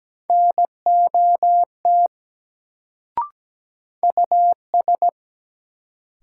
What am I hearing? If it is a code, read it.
Morse code, heard as a keyed tone, about 17 wpm, 700 Hz. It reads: NOT US
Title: Sets of 2 Words - New Word Drill - 17wpm